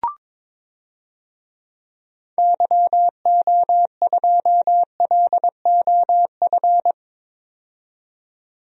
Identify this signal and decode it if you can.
Morse code, heard as a keyed tone, about 22 wpm, 700 Hz. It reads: YO2LOF
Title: DXCC-100 Call Signs - 6 Characters - 22wpm